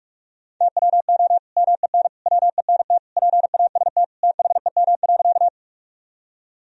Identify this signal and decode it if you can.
Morse code, heard as a keyed tone, about 46 wpm, 700 Hz. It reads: TWO MEN WENT PAST THEM.
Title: Sentences from Top 1000 Words - Review Part 3 - 45wpm